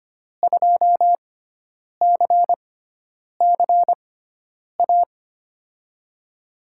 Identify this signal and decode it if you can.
Morse code, heard as a keyed tone, about 25 wpm, spaced as for 9 wpm, 700 Hz. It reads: 2CCA
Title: USA Call Signs - 6 Characters - 6x Character Spacing - 25wpm